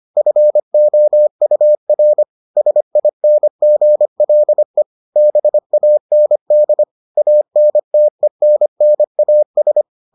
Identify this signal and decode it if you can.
Morse code, heard as a keyed tone, about 25 wpm, 600 Hz. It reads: FOUR SINGLE BAND ANTENNAS